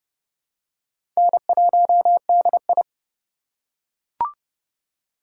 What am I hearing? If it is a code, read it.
Morse code, heard as a keyed tone, about 30 wpm, 700 Hz. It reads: N1DI
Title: USA Call Signs - 6x Character Spacing - 30wpm